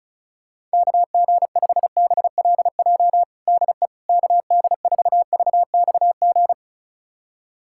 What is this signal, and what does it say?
Morse code, heard as a keyed tone, about 34 wpm, 700 Hz. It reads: KG5BLJ DE KD4VXG
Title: USA Callsign DE Callsign 35wpm